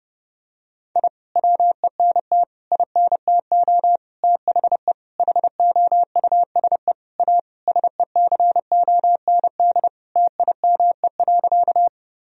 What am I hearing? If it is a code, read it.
Morse code, heard as a keyed tone, about 30 wpm, 700 Hz. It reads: I WENT INTO THE HOUSE A SECOND TIME.
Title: Sentences from Top 1000 Words - Review Part 1 - 30wpm